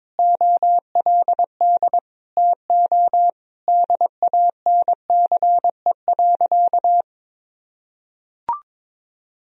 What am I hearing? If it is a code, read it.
Morse code, heard as a keyed tone, about 21 wpm, 700 Hz. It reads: OLD TO DANCE.
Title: Sentences from Top 1000 Words - Review Part 4 - 22wpm